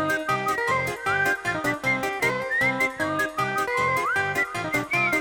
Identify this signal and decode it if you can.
Emergency Alert System (SAME) tones, heard as harsh buzzing bursts. It is heard from 1.0 to 4.5 s.